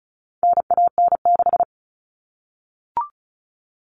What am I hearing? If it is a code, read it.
Morse code, heard as a keyed tone, about 34 wpm, 700 Hz. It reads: NAN6
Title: All Cut Numbers - 4-Digit 35wpm